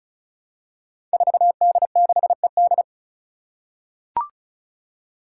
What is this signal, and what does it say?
Morse code, heard as a keyed tone, about 36 wpm, 700 Hz. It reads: 4D6ED
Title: Five Challenging Letter-Number Combinations 35wpm